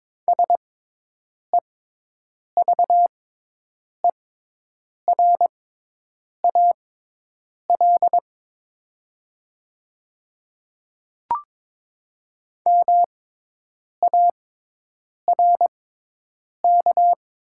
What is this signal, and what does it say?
Morse code, heard as a keyed tone, about 22 wpm, spaced as for 8 wpm, 700 Hz. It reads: SEVERAL MARK
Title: 6x Character Spacing - Top 500 Words - 22wpm